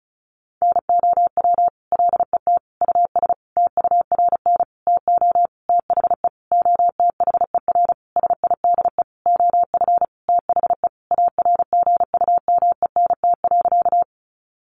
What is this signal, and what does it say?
Morse code, heard as a keyed tone, about 34 wpm, 700 Hz. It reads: NOW LET US TURN TO THE OTHER SIDE OF THE ARGUMENT.
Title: Sentences from Top 1400 Words 35wpm